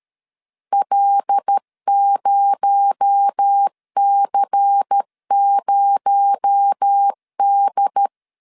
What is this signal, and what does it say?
Morse code, heard as a keyed tone, about 13 wpm, 800 Hz. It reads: L0C0D
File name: LEZIONI / 6 PKX0 / 6 PKX0 12wpm 2231